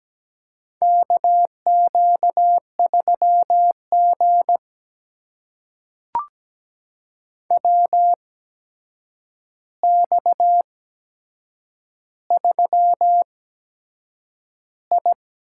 Morse code, heard as a keyed tone, about 17 wpm, spaced as for 7 wpm, 700 Hz. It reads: KQ3G WX3I
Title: USA Call Signs - 4 Characters - 8x Character Spacing - 17wpm